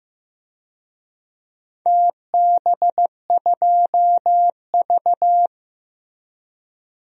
Morse code, heard as a keyed tone, about 15 wpm, 700 Hz. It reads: TB2V
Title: DXCC-100 Call Signs - 4 Characters - Repeated 3x - 15wpm